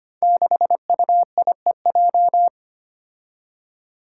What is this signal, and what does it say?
Morse code, heard as a keyed tone, about 25 wpm, 700 Hz. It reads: 6UIEJ
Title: Five Challenging Letter-Number Combinations 25wpm